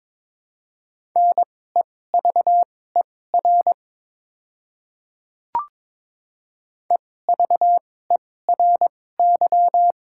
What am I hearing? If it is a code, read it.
Morse code, heard as a keyed tone, about 22 wpm, spaced as for 16 wpm, 700 Hz. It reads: NEVER EVERY
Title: 2x Character Spacing - Top 300 Words - 22wpm